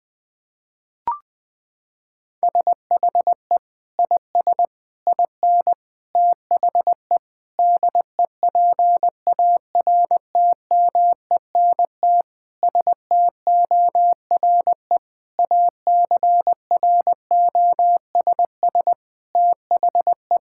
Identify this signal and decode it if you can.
Morse code, heard as a keyed tone, about 20 wpm, 700 Hz. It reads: SHE IS IN THE DEPARTMENT STORE ACROSS THE
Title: Sentences from Top 1000 Words - Review Part 5 - 20wpm